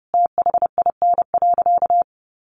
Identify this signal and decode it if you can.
Morse code, heard as a keyed tone, about 30 wpm, 700 Hz. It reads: THIN.
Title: Sentences from Top 1200 Words 30wpm